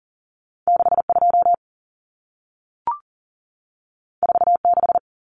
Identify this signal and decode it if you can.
Morse code, heard as a keyed tone, about 40 wpm, 700 Hz. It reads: 62 46
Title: Numbers - 2-Digit 40wpm